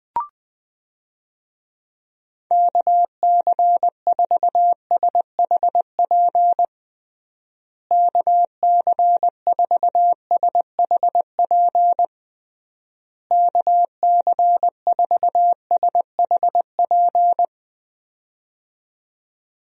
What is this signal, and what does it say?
Morse code, heard as a keyed tone, about 20 wpm, 700 Hz. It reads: KC4SHP KC4SHP KC4SHP
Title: USA Call Signs - 6 Characters - Repeated 3x - 20wpm